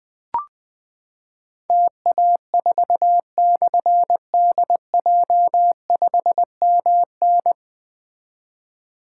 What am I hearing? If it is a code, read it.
Morse code, heard as a keyed tone, about 20 wpm, 700 Hz. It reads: TA4/DJ5MN
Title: Difficult Call Signs 20wpm